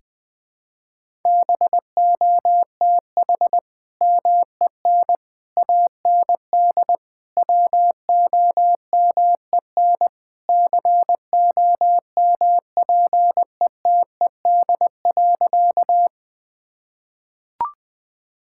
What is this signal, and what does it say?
Morse code, heard as a keyed tone, about 21 wpm, 700 Hz. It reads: BOTH MEN AND WOMEN COMPETED.
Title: Sentences with 3-Word Phrases - Spoken Beginning - 20wpm